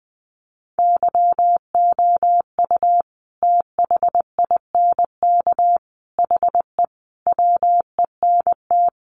Morse code, heard as a keyed tone, about 20 wpm, 700 Hz. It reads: YOU THINK HE WENT
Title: Sentences from Top 800 Words 20wpm